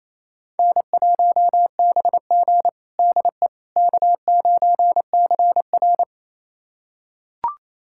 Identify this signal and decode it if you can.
Morse code, heard as a keyed tone, about 28 wpm, 700 Hz. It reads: N1BG DE K9CR
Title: USA Callsign DE Callsign - 4 Characters - 28wpm